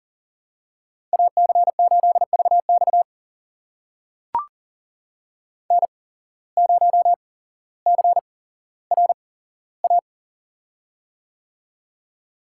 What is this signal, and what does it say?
Morse code, heard as a keyed tone, about 41 wpm, spaced as for 17 wpm, 700 Hz. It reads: AC8VX N0CRA
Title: USA Call Signs - 5 Characters - 8x Character Spacing - 40wpm